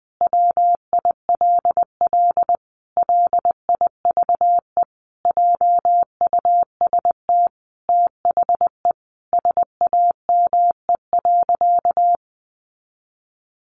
Morse code, heard as a keyed tone, about 20 wpm, 700 Hz. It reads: WILL LIVE JUST THE SAME.
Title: Sentences from Top 300 Words 20wpm